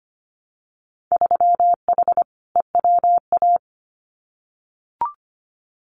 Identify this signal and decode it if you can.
Morse code, heard as a keyed tone, about 25 wpm, 700 Hz. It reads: 3H EWA
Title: Winter Field Day Exchanges Encore 25wpm